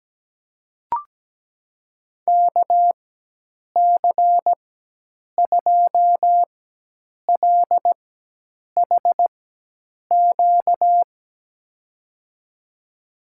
Morse code, heard as a keyed tone, about 17 wpm, spaced as for 8 wpm, 700 Hz. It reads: KC2LHQ
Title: USA Call Signs - 6 Characters - 4x Character Spacing - 17wpm